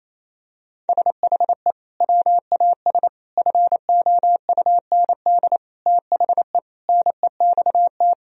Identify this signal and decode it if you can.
Morse code, heard as a keyed tone, about 27 wpm, 700 Hz. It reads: SHE WAS FOUND THE NEXT
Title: Sentences from Top 1000 Words - Review Part 3 - 28wpm